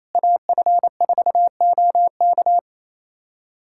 Morse code, heard as a keyed tone, about 28 wpm, 700 Hz. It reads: AF4OK
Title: USA Callsign DE Callsign - 5 Characters - 28wpm